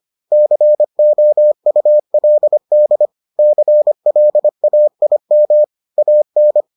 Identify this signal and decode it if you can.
Morse code, heard as a keyed tone, about 25 wpm, 600 Hz. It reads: COULD CLAIM AN